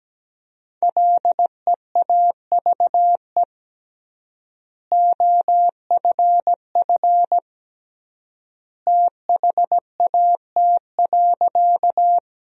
Morse code, heard as a keyed tone, about 17 wpm, 700 Hz. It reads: LEAVE OFF THAT.
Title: 3x Word-Spacing - Sentences from Top 400 Words - 17wpm